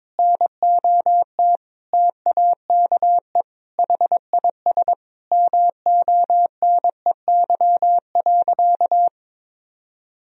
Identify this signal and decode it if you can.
Morse code, heard as a keyed tone, about 22 wpm, 700 Hz. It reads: NOT TAKE HIS MONEY.
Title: Sentences from Top 1000 Words - Review Part 4 - 22wpm